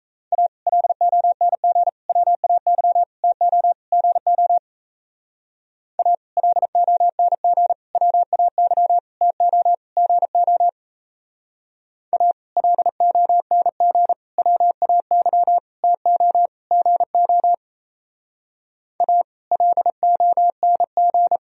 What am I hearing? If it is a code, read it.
Morse code, heard as a keyed tone, about 34 wpm, 700 Hz. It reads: A LONG WAY TO GO A LONG WAY TO GO A LONG WAY TO GO A LONG
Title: Speed-Racing - Sets of 5 Words 28wpm